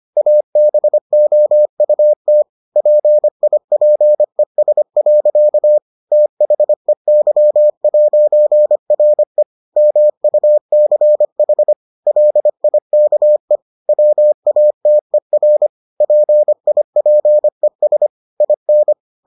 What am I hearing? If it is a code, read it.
Morse code, heard as a keyed tone, about 25 wpm, 600 Hz. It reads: ABOUT PIPES. THEY'RE MUCH LIKE WATER PIPES IN